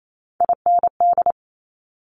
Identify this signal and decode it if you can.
Morse code, heard as a keyed tone, about 28 wpm, 700 Hz. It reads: IND
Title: Multi-letter Phonemes 28wpm